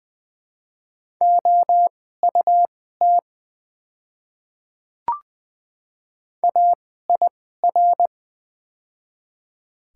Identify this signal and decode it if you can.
Morse code, heard as a keyed tone, about 20 wpm, spaced as for 14 wpm, 700 Hz. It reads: OUT AIR
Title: 2x Character Spacing - Top 300 Words - 20wpm